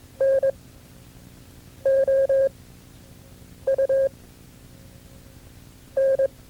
Morse code, heard as a keyed tone, about 22 wpm, spaced as for 5 wpm, 540 Hz. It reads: NOUN
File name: FreeFiles / CW-practice / KB6MT-tape2-side1